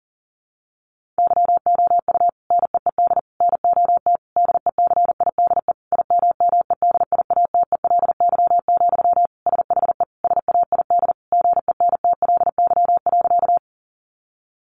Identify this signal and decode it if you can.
Morse code, heard as a keyed tone, about 40 wpm, 700 Hz. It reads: YOU NEED NOT DECIDE IMMEDIATELY, SHE SAID GENTLY.
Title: Sentences from Top 1500 Words 40wpm